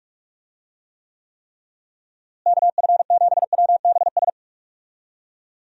Morse code, heard as a keyed tone, about 46 wpm, 700 Hz. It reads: KF7WBS
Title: USA Call Signs - 6 Characters - Repeated 3x - 45wpm